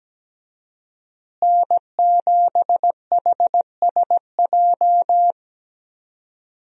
Morse code, heard as a keyed tone, about 17 wpm, 700 Hz. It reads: N7HSJ
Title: USA Call Signs - 5 Characters - Repeated 3x - 17wpm